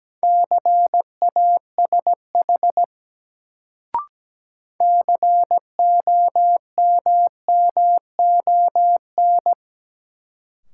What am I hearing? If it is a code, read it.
Morse code, heard as a keyed tone, about 17 wpm, 700 Hz. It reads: CASH COMMON
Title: Sentences from Top 700 Words - New Word Drill - 17wpm